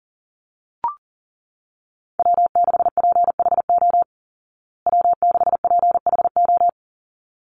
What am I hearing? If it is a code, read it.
Morse code, heard as a keyed tone, about 40 wpm, 700 Hz. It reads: W6PHO W6PHO
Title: Call Signs Repeated 3x - 40wpm